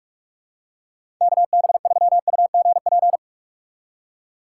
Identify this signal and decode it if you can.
Morse code, heard as a keyed tone, about 47 wpm, 700 Hz. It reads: KB3UGP
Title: USA Call Signs - 2x Character Spacing - 45wpm